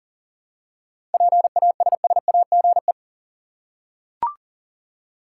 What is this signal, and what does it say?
Morse code, heard as a keyed tone, about 40 wpm, 700 Hz. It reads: PASSAGE
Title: Seven Letter Words - 40wpm